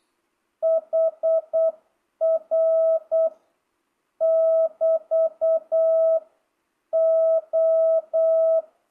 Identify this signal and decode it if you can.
Morse code, heard as a keyed tone, about 8 wpm, 660 Hz. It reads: HR <BT> O
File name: jh8cbh / 2025cw